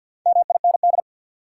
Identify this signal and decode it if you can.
Morse code, heard as a keyed tone, about 48 wpm, 700 Hz. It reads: MIND
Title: Sets of 5 Words - New Word Drill - 50wpm